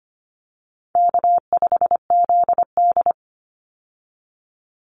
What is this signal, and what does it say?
Morse code, heard as a keyed tone, about 25 wpm, 700 Hz. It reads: K5ZD
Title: Easy Call Signs Repeated 2x - 25wpm